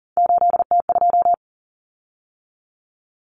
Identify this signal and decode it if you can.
Morse code, heard as a keyed tone, about 40 wpm, 700 Hz. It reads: 8T2
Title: Common Cut Numbers - 3-Digit 40wpm